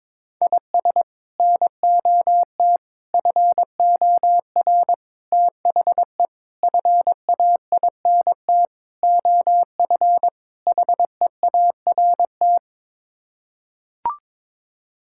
Morse code, heard as a keyed tone, about 22 wpm, 700 Hz. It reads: IS NOT FOR THE FAINT OF HEART
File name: practice / 22wpm / Sets of 7 Words - 22wpm